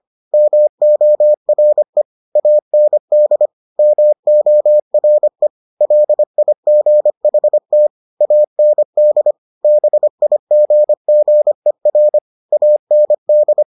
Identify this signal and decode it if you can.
Morse code, heard as a keyed tone, about 25 wpm, 600 Hz. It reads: MORE AND MORE LIGHT AND BIGGER AND